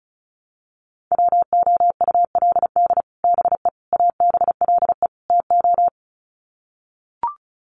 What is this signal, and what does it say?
Morse code, heard as a keyed tone, about 34 wpm, 700 Hz. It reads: WOULD BE ABLE TO
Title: Sets of 4 Words 35wpm